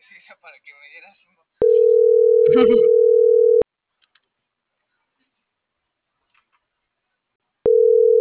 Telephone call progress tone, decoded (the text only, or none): ringback tone